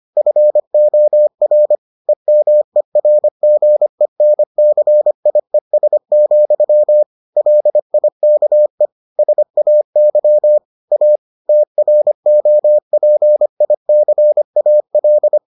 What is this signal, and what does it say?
Morse code, heard as a keyed tone, about 25 wpm, 600 Hz. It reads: FOR EMERGENCIES, LIKE SAY A TROPICAL